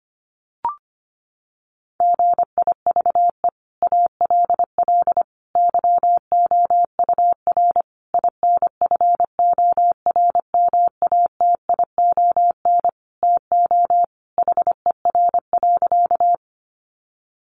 Morse code, heard as a keyed tone, about 25 wpm, 700 Hz. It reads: GIVE ALL YOUR INFORMATION TO HER.
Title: Sentences from Top 600 Words 25wpm